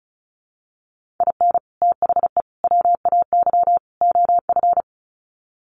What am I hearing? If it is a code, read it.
Morse code, heard as a keyed tone, about 34 wpm, 700 Hz. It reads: IN THE WAY OF
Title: Sets of 4 Words 35wpm